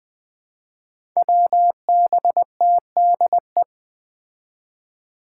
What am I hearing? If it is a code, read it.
Morse code, heard as a keyed tone, about 20 wpm, 700 Hz. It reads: WBTDE
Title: Five Challenging Letter-Number Combinations 20wpm